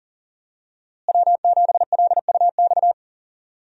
Morse code, heard as a keyed tone, about 40 wpm, 700 Hz. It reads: W7LUX
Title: USA Call Signs - 5 Characters - Repeated 3x - 40wpm